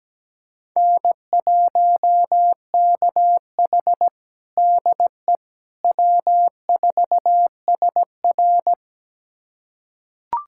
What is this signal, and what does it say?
Morse code, heard as a keyed tone, about 17 wpm, 700 Hz. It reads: N1KH DE W4SR
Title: USA Callsign DE Callsign - 4 Characters - 17wpm